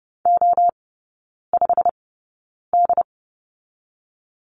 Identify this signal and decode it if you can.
Morse code, heard as a keyed tone, about 30 wpm, spaced as for 9 wpm, 700 Hz. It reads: O5D
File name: course / 30:10wpm / 087 Lesson 15e - Number 5 - Three Characters 30wpm